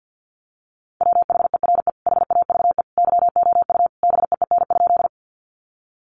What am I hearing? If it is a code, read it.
Morse code, heard as a keyed tone, about 50 wpm, 700 Hz. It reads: WHERE HAVE YOU BEEN?
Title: Sentences from Top 200 Words 50wpm